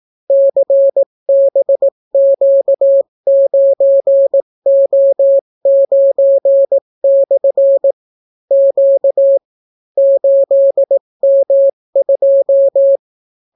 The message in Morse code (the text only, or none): CBQ9O9/ Q 8M2